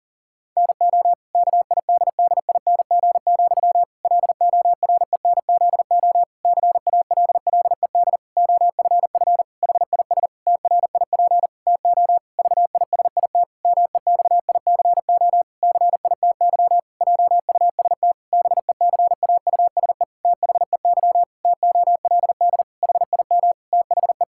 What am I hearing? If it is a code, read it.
Morse code, heard as a keyed tone, about 40 wpm, 700 Hz. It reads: NO KIDDING, LORENZO CALLED OFF HIS TRIP TO VISIT MEXICO CITY JUST BECAUSE THEY TOLD HIM THE
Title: Pangram Challenge v2 - 40wpm